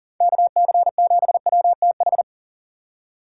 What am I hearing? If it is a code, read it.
Morse code, heard as a keyed tone, about 40 wpm, 700 Hz. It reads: KC7WTH
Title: USA Call Signs - 6 Characters - 4x Character Spacing - 40wpm